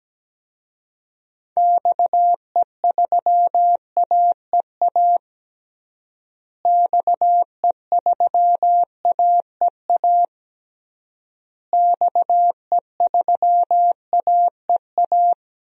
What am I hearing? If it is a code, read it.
Morse code, heard as a keyed tone, about 17 wpm, 700 Hz. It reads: XE3AEA XE3AEA XE3AEA
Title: DXCC-100 Call Signs - 6 Characters - Repeated 3x - 17wpm